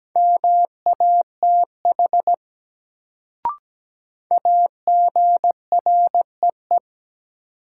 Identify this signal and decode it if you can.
Morse code, heard as a keyed tone, about 17 wpm, 700 Hz. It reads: MATH AGREE
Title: Sentences from Top 1000 Words - Review Part 6 - New Word Drill - 17wpm